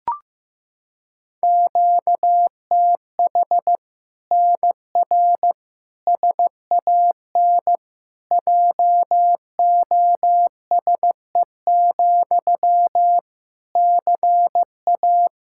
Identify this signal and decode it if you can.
Morse code, heard as a keyed tone, about 15 wpm, 700 Hz. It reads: QTH NR SAN JOSE, CA
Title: Practice QSO - Exchange 1 - Beginning - 15wpm